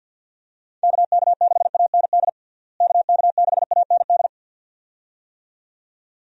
Morse code, heard as a keyed tone, about 46 wpm, 700 Hz. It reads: KK6AND KK6AND
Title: USA Call Signs - 6 Characters - Repeated 2x - 50wpm